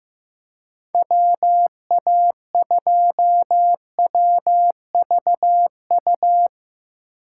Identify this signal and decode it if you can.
Morse code, heard as a keyed tone, about 17 wpm, 700 Hz. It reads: WA2WVU
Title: Speed Racing - USA Call Signs 15wpm